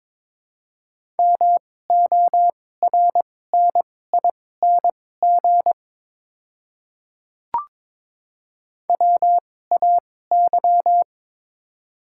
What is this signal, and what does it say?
Morse code, heard as a keyed tone, about 22 wpm, spaced as for 16 wpm, 700 Hz. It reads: MORNING WAY
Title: 2x Character Spacing - Top 500 Words - 22wpm